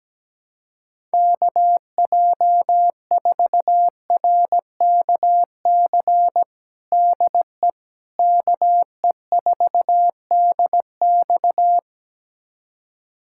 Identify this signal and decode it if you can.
Morse code, heard as a keyed tone, about 17 wpm, 700 Hz. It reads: KJ4RKC DE KE4DX
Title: USA Callsign DE Callsign 17wpm